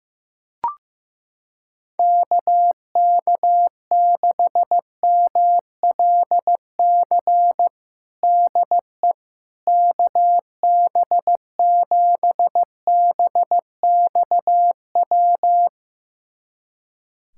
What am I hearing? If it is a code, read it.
Morse code, heard as a keyed tone, about 16 wpm, 700 Hz. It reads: KK6MLC DE KB7BXW
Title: USA Callsign DE Callsign 15wpm